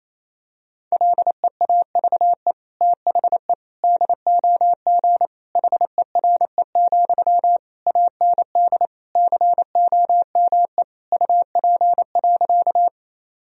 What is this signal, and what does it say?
Morse code, heard as a keyed tone, about 28 wpm, 690 Hz. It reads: LEAVE THE DOG HERE, AND COME UP.
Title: Sentences from Top 1000 Words - Review Part 2 - 28wpm